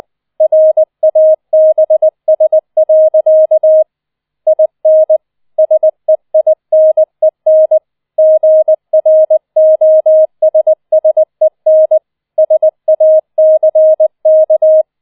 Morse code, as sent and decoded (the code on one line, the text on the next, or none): .-. .- -... ... .-.-.- / .. -. / ... .. -. -. / --. .-. --- ... ... -. / ... .- -.-. -.-
RABS. IN SEINEN GROSSEN SACK